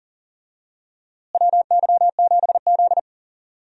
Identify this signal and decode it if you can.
Morse code, heard as a keyed tone, about 37 wpm, 700 Hz. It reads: WY7Z
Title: USA Call Signs - 4 Characters - Repeated 2x - 40wpm